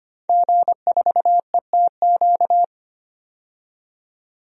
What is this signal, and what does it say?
Morse code, heard as a keyed tone, about 25 wpm, 700 Hz. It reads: G4ETQ
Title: DXCC-100 Call Signs - 5 Characters - 25wpm